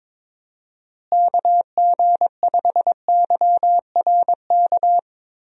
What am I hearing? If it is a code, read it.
Morse code, heard as a keyed tone, about 21 wpm, 700 Hz. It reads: KG5YRK
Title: USA Call Signs - 6 Characters - Repeated 3x - 22wpm